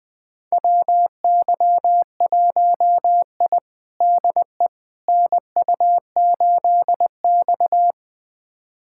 Morse code, heard as a keyed tone, about 20 wpm, 700 Hz. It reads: WY1I DE NU8X
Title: USA Callsign DE Callsign - 4 Characters - 20wpm